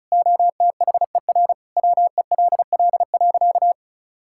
Morse code, heard as a keyed tone, about 34 wpm, 690 Hz. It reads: OTHER WELL.
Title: Sentences from Top 1000 Words - Review Part 5 - 35wpm